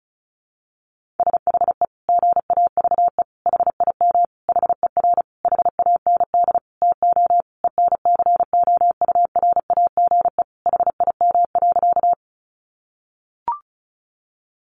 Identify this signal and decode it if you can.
Morse code, heard as a keyed tone, about 34 wpm, 700 Hz. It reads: SHE GAVE HIM HER HAND TO ENCOURAGE HIM.
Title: Sentences from Top 1400 Words 35wpm